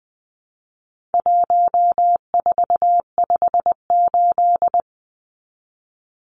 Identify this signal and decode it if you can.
Morse code, heard as a keyed tone, about 20 wpm, 700 Hz. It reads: 1458
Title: Numbers - 4-Digit 20wpm